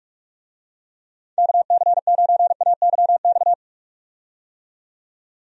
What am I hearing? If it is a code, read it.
Morse code, heard as a keyed tone, about 44 wpm, 700 Hz. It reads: KC9AYX
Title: USA Call Signs - 6 Characters - 45wpm